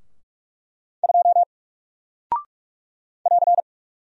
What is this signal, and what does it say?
Morse code, heard as a keyed tone, about 44 wpm, 700 Hz. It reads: 2 <AR>
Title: Single Character (Letter:Number:Punctuation:Prosign) 45wpm